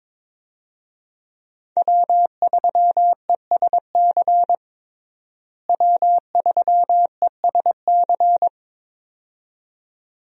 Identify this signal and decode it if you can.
Morse code, heard as a keyed tone, about 22 wpm, 700 Hz. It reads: W3ESC W3ESC
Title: USA Call Signs - 5 Characters - Repeated 2x - 22wpm